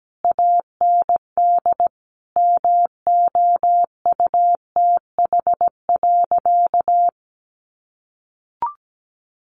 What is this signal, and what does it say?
Morse code, heard as a keyed tone, about 17 wpm, 700 Hz. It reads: AND MOUTH.
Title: Easy Phrases 17wpm